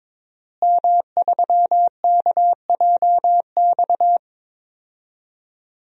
Morse code, heard as a keyed tone, about 22 wpm, 700 Hz. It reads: M3KJX
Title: DXCC-100 Call Signs - 5 Characters - Repeated 3x - 22wpm